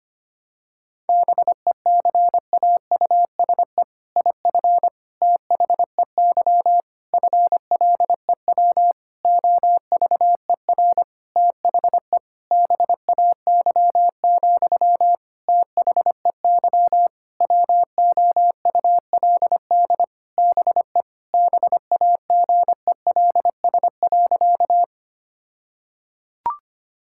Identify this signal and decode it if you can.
Morse code, heard as a keyed tone, about 25 wpm, 700 Hz. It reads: BECAUSE IF THEY FLEW OVER THE BAY, THEY WOULD BE BAGELS.
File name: practice / 25wpm / All-Age Appropriate Jokes - 25wpm